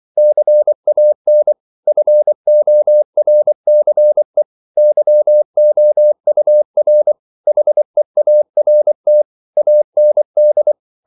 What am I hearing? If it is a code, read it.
Morse code, heard as a keyed tone, about 24 wpm, 600 Hz. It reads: CAN FORCE YOUR HEART AND
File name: CW / Practice / If24